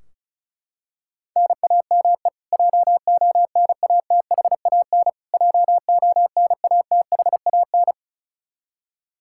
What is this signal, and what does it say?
Morse code, heard as a keyed tone, about 34 wpm, 700 Hz. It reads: NAME JONATHAN JONATHAN
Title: Practice QSO - Exchange 1 - Beginning - 35wpm